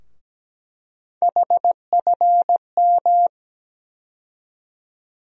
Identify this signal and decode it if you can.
Morse code, heard as a keyed tone, about 17 wpm, 700 Hz. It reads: HFM
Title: USA Call Sign Postfixes - 3 Letters 17wpm